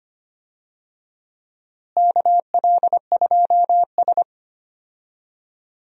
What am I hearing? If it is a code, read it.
Morse code, heard as a keyed tone, about 25 wpm, 700 Hz. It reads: KL2S
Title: USA Call Signs - 4 Characters - Repeated 2x - 25wpm